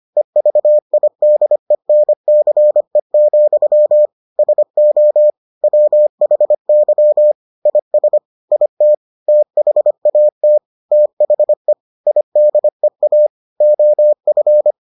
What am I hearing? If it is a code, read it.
Morse code, heard as a keyed tone, about 25 wpm, 600 Hz. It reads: EVIDENCE, SO WHY IS IT THAT THE IDEA OF